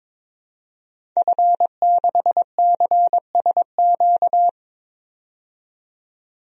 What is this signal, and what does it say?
Morse code, heard as a keyed tone, about 22 wpm, 700 Hz. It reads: F6CSQ
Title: DXCC-100 Call Signs - 5 Characters - 22wpm